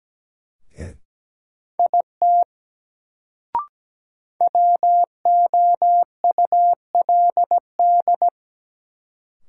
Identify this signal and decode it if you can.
Morse code, heard as a keyed tone, about 17 wpm, 700 Hz. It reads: IT WOULD
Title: Sentences from Top 100 Words - New Word Drill - 17wpm